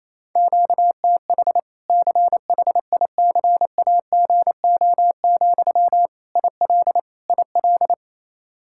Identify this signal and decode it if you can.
Morse code, heard as a keyed tone, about 28 wpm, 700 Hz. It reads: QTH CHICAGO, IL IL
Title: Practice QSO - Exchange 1 - Beginning - 28wpm